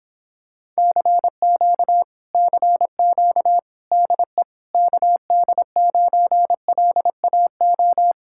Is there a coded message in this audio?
Yes: Morse code from 0.8 to 8.1 s, heard as a keyed tone, about 26 wpm, 700 Hz. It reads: CQ CQ DE KD9LAO